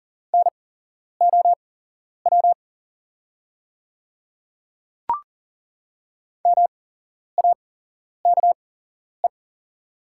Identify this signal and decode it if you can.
Morse code, heard as a keyed tone, about 40 wpm, spaced as for 11 wpm, 700 Hz. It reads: NOW MAKE
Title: 8x Character Spacing - Top 200 Words - 40wpm